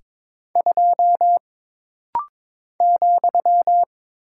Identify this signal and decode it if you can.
Morse code, heard as a keyed tone, about 22 wpm, 700 Hz. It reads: 2 ,
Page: This rapid-fire timing is geared to push your ICR (Instant Character Recognition) skills.